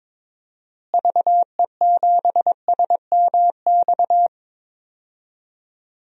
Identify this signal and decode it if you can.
Morse code, heard as a keyed tone, about 22 wpm, 700 Hz. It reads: VE7SMX
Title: DXCC-100 Call Signs - 6 Characters - Repeated 2x - 22wpm